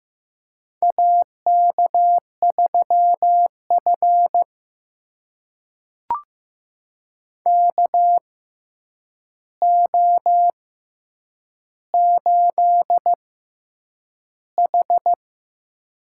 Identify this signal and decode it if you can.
Morse code, heard as a keyed tone, about 15 wpm, spaced as for 8 wpm, 700 Hz. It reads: AK3F KO8H
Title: USA Call Signs - 6x Character Spacing - 15wpm